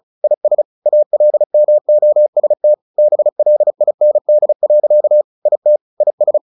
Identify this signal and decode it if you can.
Morse code, heard as a keyed tone, about 34 wpm, 600 Hz. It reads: IS ALMOST BLIND. IT IS
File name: the old man and the sea / the old man and the sea 1 35wpm